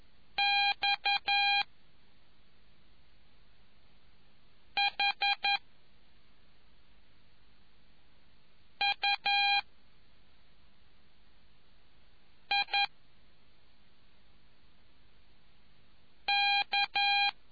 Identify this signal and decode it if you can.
Morse code, heard as a keyed tone, about 11 wpm, 2400 Hz. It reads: X H U I K